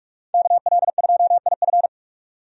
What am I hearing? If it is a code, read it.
Morse code, heard as a keyed tone, about 41 wpm, 700 Hz. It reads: KL2IF
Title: USA Call Signs - 5 Characters - 45wpm